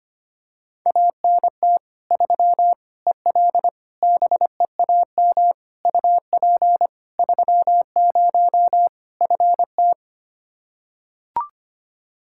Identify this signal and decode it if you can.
Morse code, heard as a keyed tone, about 25 wpm, 700 Hz. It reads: ANT 3 EL BEAM UP 30 FT
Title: Practice QSO - Exchange 2 - Info - 25wpm